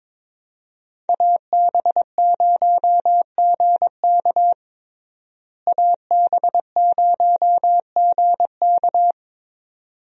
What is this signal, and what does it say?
Morse code, heard as a keyed tone, about 22 wpm, 700 Hz. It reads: AB0GK AB0GK
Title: USA Call Signs - 5 Characters - Repeated 2x - 22wpm